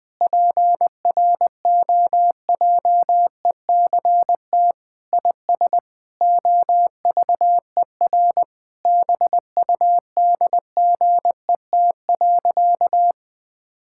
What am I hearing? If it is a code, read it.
Morse code, heard as a keyed tone, about 20 wpm, 700 Hz. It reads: PROJECT IS OVER BUDGET.
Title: Sentences with 8-Word Phrases - 20wpm